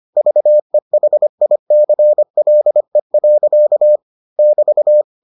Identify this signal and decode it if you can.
Morse code, heard as a keyed tone, about 25 wpm, 600 Hz. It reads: VEHICLE. <BT>